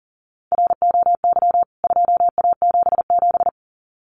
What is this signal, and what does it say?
Morse code, heard as a keyed tone, about 40 wpm, 700 Hz. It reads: ROY 2A77
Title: CWT Exchange 40wpm